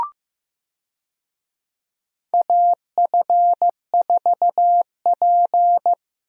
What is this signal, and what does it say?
Morse code, heard as a keyed tone, about 14 wpm, 700 Hz. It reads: AF4P
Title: USA Call Signs - 4 Characters - Repeated 3x - 15wpm